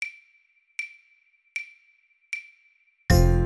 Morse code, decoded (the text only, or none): none